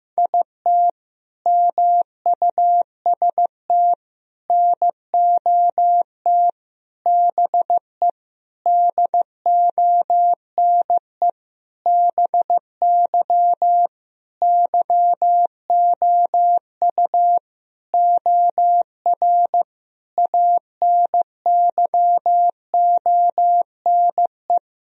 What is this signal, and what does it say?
Morse code, heard as a keyed tone, about 15 wpm, 700 Hz. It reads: IT MUST NOT BE DONE BY YOU OR ANYONE
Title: Sentences from Top 1000 Words - Review Part 1 - 15wpm